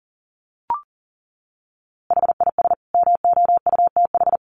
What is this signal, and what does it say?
Morse code, heard as a keyed tone, about 37 wpm, 700 Hz. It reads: HIS MOUTH
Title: A Sound Of Thunder 40wpm